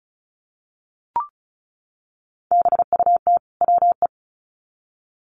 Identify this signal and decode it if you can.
Morse code, heard as a keyed tone, about 34 wpm, 700 Hz. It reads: BUT WE
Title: Sets of 2 Words 35wpm